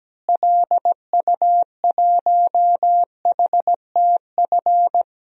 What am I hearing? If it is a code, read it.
Morse code, heard as a keyed tone, about 17 wpm, 700 Hz. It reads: LU1HTF
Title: DXCC-100 Call Signs - 6 Characters - Repeated 2x - 17wpm